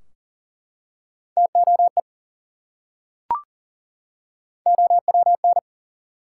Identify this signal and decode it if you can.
Morse code, heard as a keyed tone, about 38 wpm, 700 Hz. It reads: TOE OWN
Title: Three Letter Words - 40wpm